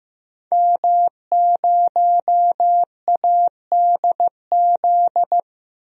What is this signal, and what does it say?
Morse code, heard as a keyed tone, about 15 wpm, 700 Hz. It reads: M0ADZ
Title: DXCC-100 Call Signs - 5 Characters - Repeated 3x - 15wpm